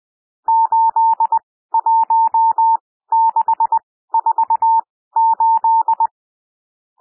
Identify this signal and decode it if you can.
Morse code, heard as a keyed tone, about 20 wpm, spaced as for 14 wpm, 900 Hz. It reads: 81648
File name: cwmp3 / N1 SOUND NUMBERS 20 wpm